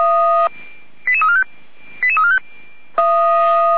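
Five-tone selective call (ZVEI), heard as quick runs of five tones, tones of about 70 ms. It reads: 88345 88345
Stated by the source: ZVEI five-tone call 88289 received by SDR.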